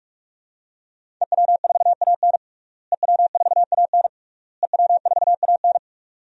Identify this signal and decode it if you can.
Morse code, heard as a keyed tone, about 43 wpm, 700 Hz. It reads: EW4AN EW4AN EW4AN
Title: DXCC-100 Call Signs - 5 Characters - Repeated 3x - 45wpm